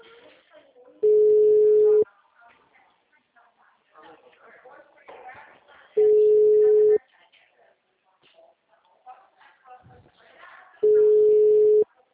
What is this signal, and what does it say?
Ringback tone (Europe, 425 Hz), with speech.
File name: out- -2001- - - .59013